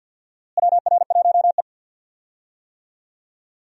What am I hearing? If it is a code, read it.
Morse code, heard as a keyed tone, about 49 wpm, 700 Hz. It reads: WR1E
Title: USA Call Signs - 4 Characters - Repeated 3x - 50wpm